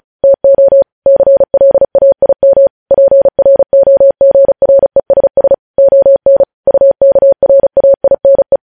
Morse code, heard as a keyed tone, about 36 wpm, 550 Hz. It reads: TO CLAIM PROGRESS ON UKRAINE
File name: podcast / News headlines-1x-35x35